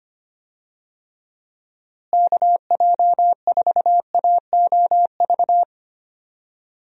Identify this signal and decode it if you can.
Morse code, heard as a keyed tone, about 25 wpm, 700 Hz. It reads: KJ4AOV
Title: USA Call Signs - 6 Characters - Repeated 2x - 25wpm